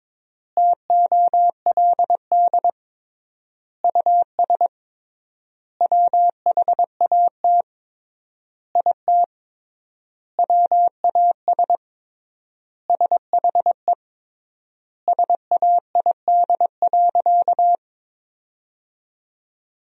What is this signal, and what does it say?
Morse code, heard as a keyed tone, about 22 wpm, 700 Hz. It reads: TOLD US WHAT IT WAS SHE SAID.
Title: 3x Word-Spacing - Sentences from Top 400 Words - 22wpm